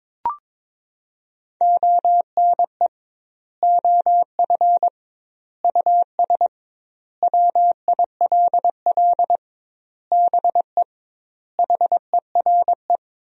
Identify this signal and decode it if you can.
Morse code, heard as a keyed tone, about 22 wpm, 700 Hz. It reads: ONE OF US WILL BE HERE
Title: 2x Word-Spacing - Sentences from Top 300 Words - 22wpm